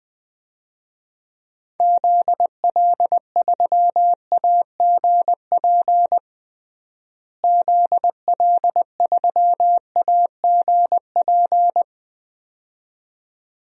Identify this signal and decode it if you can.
Morse code, heard as a keyed tone, about 20 wpm, 700 Hz. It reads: ZL3AGP ZL3AGP
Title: DXCC-100 Call Signs - 6 Characters - Repeated 2x - 20wpm